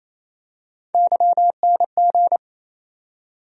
Morse code, heard as a keyed tone, about 28 wpm, 700 Hz. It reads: YNG
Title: USA Call Sign Postfixes - 3 Letters 28wpm